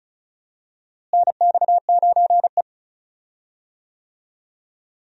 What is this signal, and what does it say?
Morse code, heard as a keyed tone, about 36 wpm, 700 Hz. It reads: NX9E
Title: USA Call Signs - 4 Characters - Repeated 3x - 35wpm